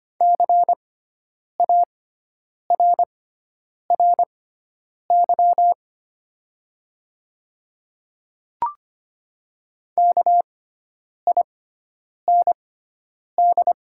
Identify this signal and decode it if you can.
Morse code, heard as a keyed tone, about 25 wpm, spaced as for 9 wpm, 700 Hz. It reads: CARRY KIND
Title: 6x Character Spacing - Top 500 Words - 25wpm